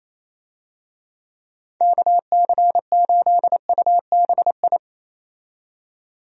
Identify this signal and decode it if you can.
Morse code, heard as a keyed tone, about 28 wpm, 700 Hz. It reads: KC8UBI
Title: USA Call Signs - 6 Characters - 28wpm